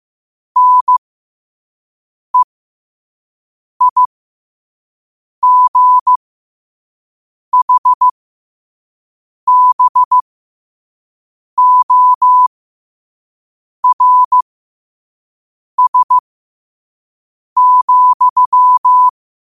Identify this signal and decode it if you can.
Morse code, heard as a keyed tone, about 15 wpm, spaced as for 5 wpm, 1000 Hz. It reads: NEIGHBORS,
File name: qotdfiles / 5wpm